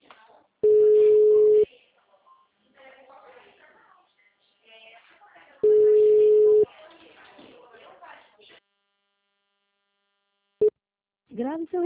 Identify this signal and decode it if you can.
Ringback tone (Europe, 425 Hz), with speech.